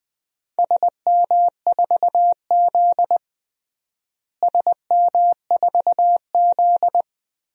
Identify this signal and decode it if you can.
Morse code, heard as a keyed tone, about 20 wpm, 700 Hz. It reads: SM4Z SM4Z
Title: DXCC-100 Call Signs - 4 Characters - Repeated 2x - 20wpm